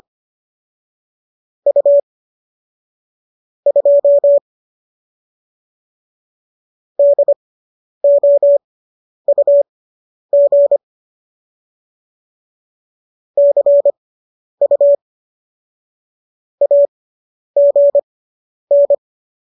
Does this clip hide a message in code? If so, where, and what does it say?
Yes: Morse code from 1.7 to 18.9 s, heard as a keyed tone, about 25 wpm, spaced as for 10 wpm, 570 Hz. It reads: U 2 DOUG CU AGN